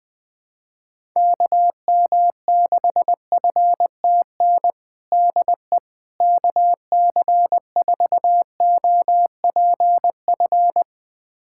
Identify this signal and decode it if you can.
Morse code, heard as a keyed tone, about 20 wpm, 700 Hz. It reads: KM6FTN DE KC4OPF
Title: USA Callsign DE Callsign 20wpm